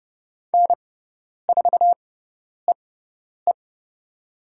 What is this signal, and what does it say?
Morse code, heard as a keyed tone, about 30 wpm, spaced as for 10 wpm, 700 Hz. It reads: N4EE
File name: Lesson 17g - Letter u - Call Signs 30wpm